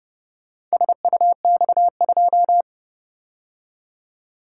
Morse code, heard as a keyed tone, about 30 wpm, 700 Hz. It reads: SUX2